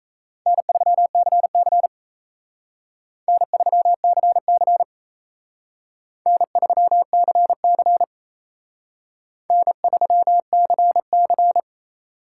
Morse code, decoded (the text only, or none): N3CC N3CC N3CC N3CC